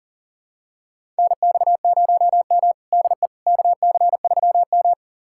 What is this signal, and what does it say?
Morse code, heard as a keyed tone, about 40 wpm, 700 Hz. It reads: NX0M DE KC3M
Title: USA Callsign DE Callsign - 4 Characters - 40wpm